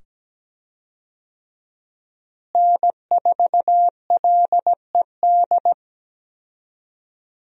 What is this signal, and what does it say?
Morse code, heard as a keyed tone, about 17 wpm, 700 Hz. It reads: N4LED
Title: USA Call Signs - 5 Characters - Repeated 2x - 17wpm